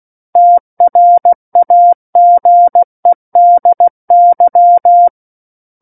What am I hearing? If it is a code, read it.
Morse code, heard as a keyed tone, about 16 wpm, 700 Hz. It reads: TRAGEDY
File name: Book One / 07-Chapter 7 - 16 WPM